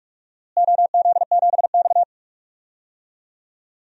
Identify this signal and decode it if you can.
Morse code, heard as a keyed tone, about 42 wpm, 700 Hz. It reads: OZ7X
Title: DXCC-100 Call Signs - 4 Characters - Repeated 3x - 45wpm